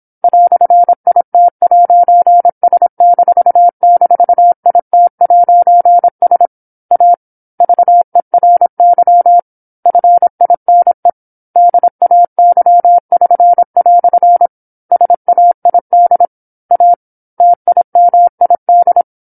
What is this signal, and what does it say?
Morse code, heard as a keyed tone, about 27 wpm, 700 Hz. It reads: "IT'S--IT'S A VERY FINE DAY<SN>" SAID A TIMID